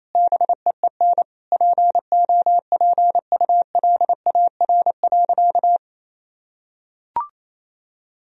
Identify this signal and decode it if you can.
Morse code, heard as a keyed tone, about 28 wpm, 700 Hz. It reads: BEEN POPULAR.
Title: Sentences from Top 1000 Words - Review Part 3 - 28wpm